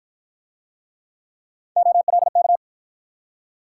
Morse code, heard as a keyed tone, about 50 wpm, 700 Hz. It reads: OLK